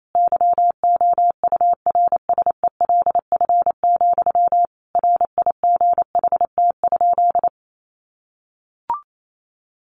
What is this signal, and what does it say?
Morse code, heard as a keyed tone, about 28 wpm, 700 Hz. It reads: YOURSELF, RIGHT?
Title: Sentences from Top 1200 Words 28wpm